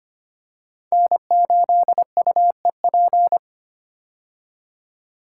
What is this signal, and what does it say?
Morse code, heard as a keyed tone, about 25 wpm, 700 Hz. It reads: N8UEP
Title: Code Groups - 5 Letter:Numbers - 25wpm